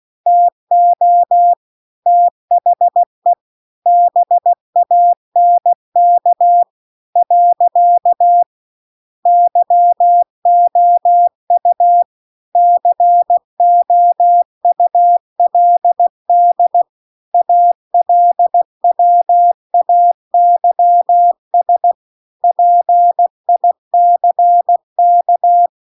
Morse code, heard as a keyed tone, about 16 wpm, 700 Hz. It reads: TO THE BANK . YOU COULD ALWAYS PICK